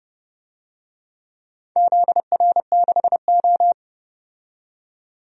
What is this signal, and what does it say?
Morse code, heard as a keyed tone, about 30 wpm, 700 Hz. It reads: ZR6O
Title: DXCC-100 Call Signs - 4 Characters - 30wpm